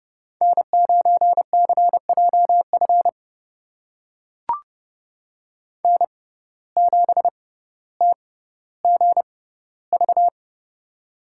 Morse code, heard as a keyed tone, about 30 wpm, spaced as for 15 wpm, 700 Hz. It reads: N9CJF N7TGV
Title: USA Call Signs - 5 Characters - 6x Character Spacing - 30wpm